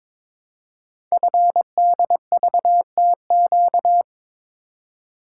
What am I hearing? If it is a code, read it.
Morse code, heard as a keyed tone, about 22 wpm, 700 Hz. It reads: FDVTQ
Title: Code Groups - 5 Letters - 22wpm